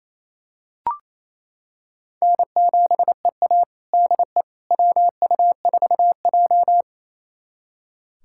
Morse code, heard as a keyed tone, about 27 wpm, 700 Hz. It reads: N7EA DE WU4J
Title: USA Callsign DE Callsign - 4 Characters - 28wpm